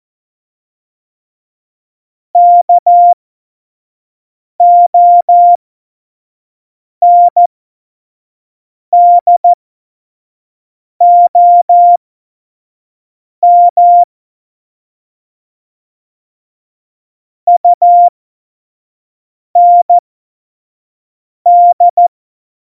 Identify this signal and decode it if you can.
Morse code, heard as a keyed tone, about 14 wpm, spaced as for 5 wpm, 700 Hz. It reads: KONDOM UND
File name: Morsekurs / Witze / Witz 21 April 2020, 25 F70